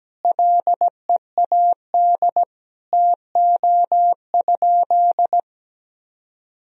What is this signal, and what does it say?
Morse code, heard as a keyed tone, about 17 wpm, 700 Hz. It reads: LEAD TO?
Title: Sentences from Top 1000 Words - Review Part 2 - 17wpm